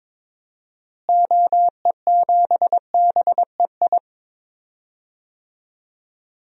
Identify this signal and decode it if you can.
Morse code, heard as a keyed tone, about 22 wpm, 700 Hz. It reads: OE7BEI
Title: DXCC-100 Call Signs - 6 Characters - Repeated 2x - 22wpm